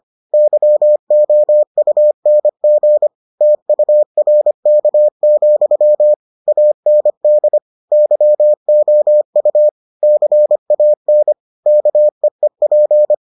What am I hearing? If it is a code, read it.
Morse code, heard as a keyed tone, about 25 wpm, 600 Hz. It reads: YOUNG TURK, AND YOU CAN KEEP